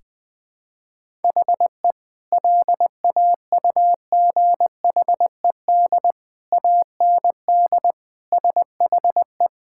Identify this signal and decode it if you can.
Morse code, heard as a keyed tone, about 20 wpm, 700 Hz. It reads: HE LAUGHED AND SHE
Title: Sentences with 2-Word Phrases - Spoken Beginning - 20wpm